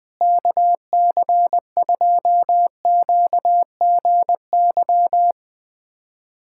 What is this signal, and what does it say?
Morse code, heard as a keyed tone, about 20 wpm, 700 Hz. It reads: KC2QGY
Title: USA Callsign DE Callsign - 6 Characters - 20wpm